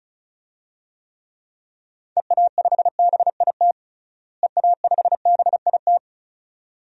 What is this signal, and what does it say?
Morse code, heard as a keyed tone, about 34 wpm, 700 Hz. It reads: EA5BIT EA5BIT
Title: DXCC-100 Call Signs - 6 Characters - Repeated 2x - 35wpm